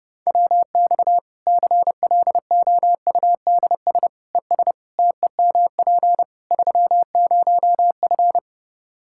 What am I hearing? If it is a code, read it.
Morse code, heard as a keyed tone, about 32 wpm, 700 Hz. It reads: WX CLOUDS ES TEMP 30F
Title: K1USN SST QSO Practice 30wpm